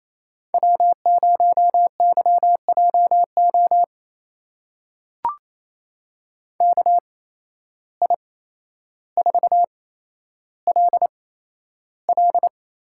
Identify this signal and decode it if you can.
Morse code, heard as a keyed tone, about 28 wpm, spaced as for 12 wpm, 700 Hz. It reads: W0YJO KI4LL
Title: USA Call Signs - 8x Character Spacing - 28wpm